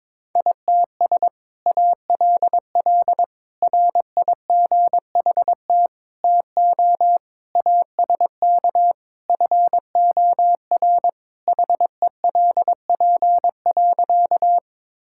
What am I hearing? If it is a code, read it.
Morse code, heard as a keyed tone, about 21 wpm, 700 Hz. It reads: ITS ALL RIGHT TO ASK FOR HELP.
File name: practice / 22wpm / Sentences with 2-Word Phrases - 22wpm